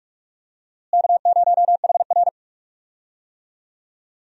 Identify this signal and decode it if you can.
Morse code, heard as a keyed tone, about 46 wpm, 700 Hz. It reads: K0HR
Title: USA Call Signs - 4 Characters - Repeated 3x - 45wpm